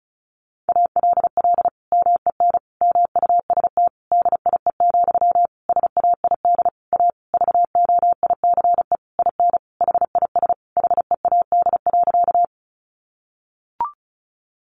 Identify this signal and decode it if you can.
Morse code, heard as a keyed tone, about 34 wpm, 700 Hz. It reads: ALL MEN MUST DIE, SAID A VOICE IN HIS HEAD.
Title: Sentences from Top 1300 Words 35wpm